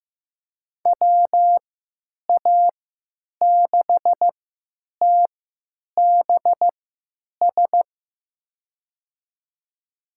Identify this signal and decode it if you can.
Morse code, heard as a keyed tone, about 15 wpm, spaced as for 8 wpm, 700 Hz. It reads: WA6TBS
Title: USA Call Signs - 6 Characters - 3x Character Spacing - 15wpm